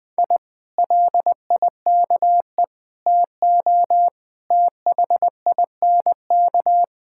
Morse code, heard as a keyed tone, about 20 wpm, 700 Hz. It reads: I LIKE TO THINK
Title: Sentences with 8-Word Phrases - 20wpm